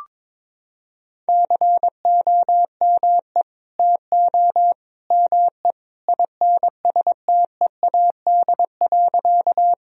Morse code, heard as a keyed tone, about 22 wpm, 700 Hz. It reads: COME TO ME INSTEAD.
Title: Sentences from Top 1000 Words - Review Part 4 - 22wpm